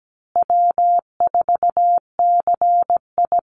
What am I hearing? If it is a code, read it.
Morse code, heard as a keyed tone, about 17 wpm, 700 Hz. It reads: W4CI
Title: Call Signs Repeated 3x - 17wpm